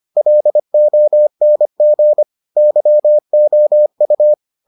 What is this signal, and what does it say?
Morse code, heard as a keyed tone, about 25 wpm, 600 Hz. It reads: LONG YOU